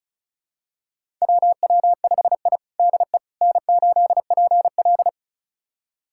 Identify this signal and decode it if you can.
Morse code, heard as a keyed tone, about 36 wpm, 700 Hz. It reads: WW5I DE N8PL
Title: USA Callsign DE Callsign - 4 Characters - 35wpm